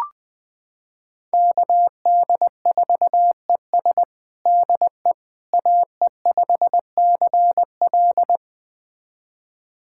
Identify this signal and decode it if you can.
Morse code, heard as a keyed tone, about 21 wpm, 700 Hz. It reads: KD4ES DE AE5CL
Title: USA Callsign DE Callsign - 5 Characters - 20wpm